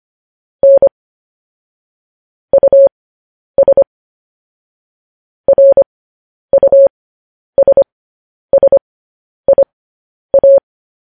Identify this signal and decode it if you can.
Morse code, heard as a keyed tone, about 25 wpm, spaced as for 10 wpm, 550 Hz. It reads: N US RUSSIA
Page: news headlines in morse code at 10 wpm